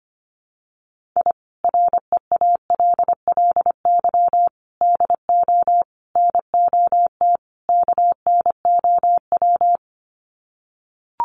Morse code, heard as a keyed tone, about 25 wpm, 700 Hz. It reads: I REALLY DO NOT KNOW
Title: Sets of 5 Words Encore 25wpm